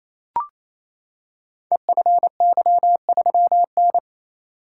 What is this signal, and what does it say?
Morse code, heard as a keyed tone, about 27 wpm, 700 Hz. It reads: EFY3N
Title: Code Groups - 5 Letter:Numbers - 28wpm